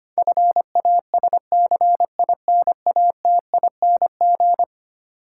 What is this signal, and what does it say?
Morse code, heard as a keyed tone, about 25 wpm, 700 Hz. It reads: FASCINATING
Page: morse-code-ninja-practice-25wpm